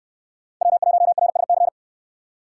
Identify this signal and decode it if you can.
Morse code, heard as a keyed tone, about 67 wpm, 700 Hz. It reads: W1RSC